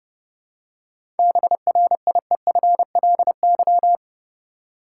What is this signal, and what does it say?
Morse code, heard as a keyed tone, about 30 wpm, 700 Hz. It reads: BRIEFLY
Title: Seven Letter Words - 30wpm